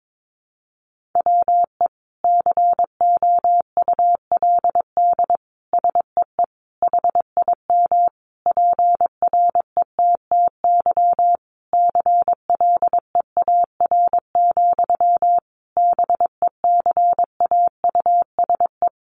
Morse code, heard as a keyed tone, about 22 wpm, 700 Hz. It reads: WE COULD SEE HIM PRETTY CLEAR, BECAUSE
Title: Sentences from Top 1000 Words 22wpm